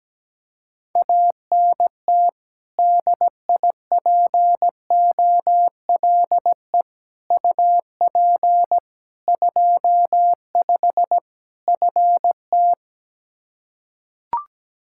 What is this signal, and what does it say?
Morse code, heard as a keyed tone, about 17 wpm, 700 Hz. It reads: ANT DIPOLE UP 25 FT
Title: K1USN SST QSO Practice 17wpm